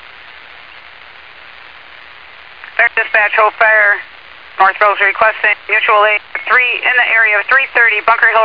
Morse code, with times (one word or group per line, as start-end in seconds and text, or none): none